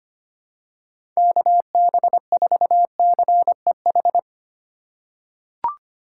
Morse code, heard as a keyed tone, about 25 wpm, 700 Hz. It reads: KB4CEH
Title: USA Call Signs - 6 Characters - 2x Character Spacing - 25wpm